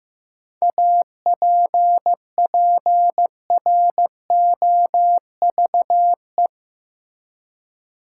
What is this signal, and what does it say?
Morse code, heard as a keyed tone, about 15 wpm, 700 Hz. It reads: APPROVE
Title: Seven Letter Words - 15wpm